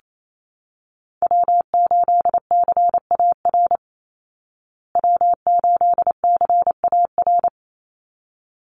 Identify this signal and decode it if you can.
Morse code, heard as a keyed tone, about 28 wpm, 690 Hz. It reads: W8CAR W8CAR
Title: Call Signs Repeated 3x - 28wpm